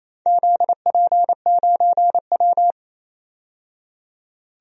Morse code, heard as a keyed tone, about 29 wpm, 690 Hz. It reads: ZP9W
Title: DXCC-100 Call Signs - 4 Characters - 28wpm